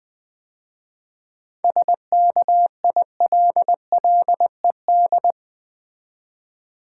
Morse code, heard as a keyed tone, about 20 wpm, 700 Hz. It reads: SKILLED
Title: 7 Letter Words - First 3 Letters Given - 20wpm